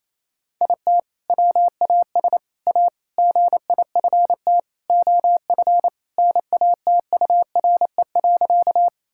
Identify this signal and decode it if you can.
Morse code, heard as a keyed tone, about 27 wpm, 700 Hz. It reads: IT WAS A GIFT OF NATURE.
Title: Sentences from Top 1000 Words - Review Part 6 - 28wpm